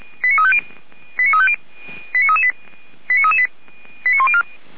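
Five-tone selective call (ZVEI), heard as quick runs of five tones, tones of about 70 ms. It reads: 89350 89338 89338 89104
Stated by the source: ZVEI five-tone call 89164 received by SDR, recorded by feuerwehr-orlamuende.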